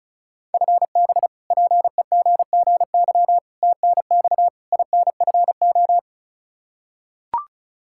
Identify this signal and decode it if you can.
Morse code, heard as a keyed tone, about 36 wpm, 700 Hz. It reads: FB PEGGY TNX INFO
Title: Practice QSO - Exchange 3 - More Info - 35wpm